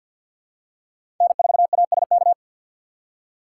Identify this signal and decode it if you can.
Morse code, heard as a keyed tone, about 50 wpm, 700 Hz. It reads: N4ASK
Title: USA Call Signs - 5 Characters - 3x Character Spacing - 50wpm